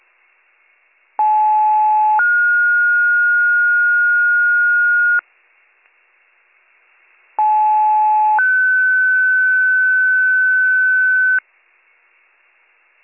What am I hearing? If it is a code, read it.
Two-tone page, heard as a one-second tone, then a three-second tone, about 860 then 1430 Hz, about 860 then 1500 Hz. It is heard from 1.2 to 11.4 s.